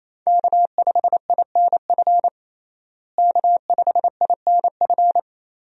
Morse code, heard as a keyed tone, about 28 wpm, 700 Hz. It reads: K5INF K5INF
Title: USA Call Signs - 5 Characters - Repeated 3x - 28wpm